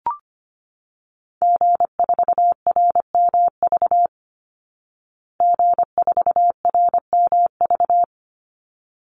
Morse code, heard as a keyed tone, about 25 wpm, 700 Hz. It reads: G4RMV G4RMV
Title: Call Signs Repeated 3x - 25wpm